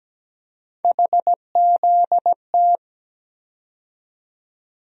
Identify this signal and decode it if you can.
Morse code, heard as a keyed tone, about 17 wpm, 700 Hz. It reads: HZT
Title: USA Call Sign Postfixes - 3 Letters 17wpm